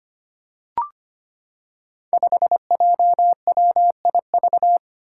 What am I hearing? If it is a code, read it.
Morse code, heard as a keyed tone, about 25 wpm, 700 Hz. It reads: 5JWIV
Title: Five Challenging Letter-Number Combinations 25wpm